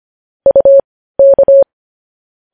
Morse code, heard as a keyed tone, about 25 wpm, spaced as for 15 wpm, 550 Hz. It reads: UK
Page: news headlines in morse code at 15 wpm